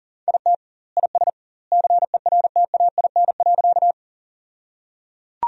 Morse code, heard as a keyed tone, about 37 wpm, 700 Hz. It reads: IT IS CERTAIN.
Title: 2x Word-Spacing - Sentences from Top 500 Words - 40wpm